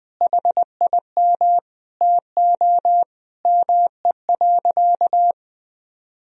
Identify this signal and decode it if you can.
Morse code, heard as a keyed tone, about 20 wpm, 700 Hz. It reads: HIM TO ME.
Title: Sentences from Top 200 Words and Abbreviations 20wpm